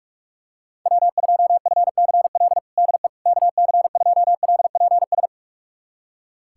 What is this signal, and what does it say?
Morse code, heard as a keyed tone, about 46 wpm, 700 Hz. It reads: W2FCL DE KC2LPS
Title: USA Callsign DE Callsign 45wpm